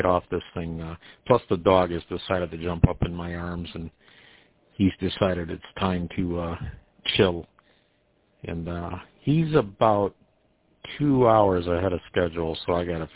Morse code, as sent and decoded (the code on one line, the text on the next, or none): none